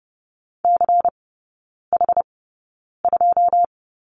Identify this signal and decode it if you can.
Morse code, heard as a keyed tone, about 30 wpm, spaced as for 9 wpm, 690 Hz. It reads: CH2
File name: Lesson 16e - Letter c - Three Characters 30wpm